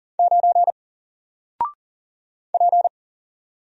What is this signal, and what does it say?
Morse code, heard as a keyed tone, about 40 wpm, 700 Hz. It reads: 9 P
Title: Single Character (Letter:Number:Punctuation:Prosign) 40wpm